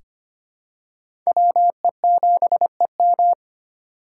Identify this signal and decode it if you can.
Morse code, heard as a keyed tone, about 25 wpm, 700 Hz. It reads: WE7EM
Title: USA Call Signs - 5 Characters - 2x Character Spacing - 25wpm